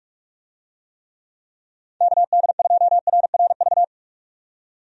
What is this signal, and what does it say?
Morse code, heard as a keyed tone, about 46 wpm, 700 Hz. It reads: KD2RRV
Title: USA Call Signs - 6 Characters - 45wpm